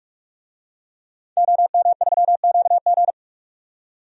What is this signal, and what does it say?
Morse code, heard as a keyed tone, about 46 wpm, 700 Hz. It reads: OM3QG
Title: DXCC-100 Call Signs - 5 Characters - 45wpm